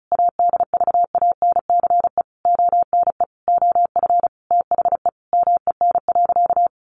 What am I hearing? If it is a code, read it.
Morse code, heard as a keyed tone, about 34 wpm, 700 Hz. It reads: ADVANCE ONE OF THE MEN.
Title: Sentences from Top 1400 Words 35wpm